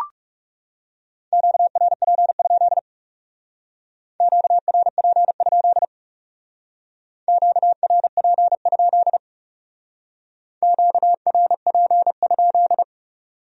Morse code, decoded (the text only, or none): QRP? QRP? QRP? QRP?